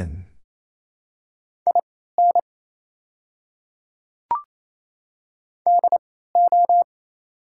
Morse code, heard as a keyed tone, about 28 wpm, spaced as for 16 wpm, 690 Hz. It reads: IN DO
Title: 3x Character Spacing - Top 500 Words - 28wpm